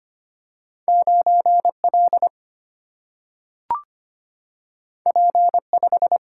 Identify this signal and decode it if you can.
Morse code, heard as a keyed tone, about 25 wpm, 700 Hz. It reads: 9L P5
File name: Two Letter-Number Combinations - 25wpm